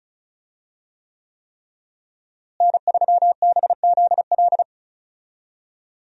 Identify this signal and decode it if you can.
Morse code, heard as a keyed tone, about 36 wpm, 700 Hz. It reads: N3BZL
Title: USA Call Signs - 5 Characters - Repeated 3x - 35wpm